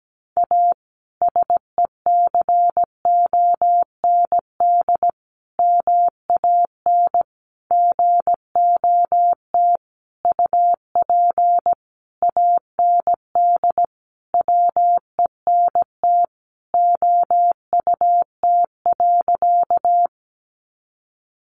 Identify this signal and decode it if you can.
Morse code, heard as a keyed tone, about 17 wpm, 700 Hz. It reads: A SECOND MAN GOT UP AND WENT OUT.
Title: Sentences from Top 300 Words 17wpm